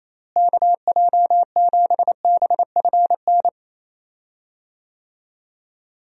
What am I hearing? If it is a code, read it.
Morse code, heard as a keyed tone, about 27 wpm, 700 Hz. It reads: KJ7BFN
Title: USA Call Signs - 6 Characters - Repeated 3x - 28wpm